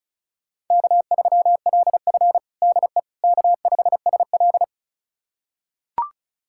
Morse code, heard as a keyed tone, about 36 wpm, 700 Hz. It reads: K3LF DE K5SL
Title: USA Callsign DE Callsign - 4 Characters - 35wpm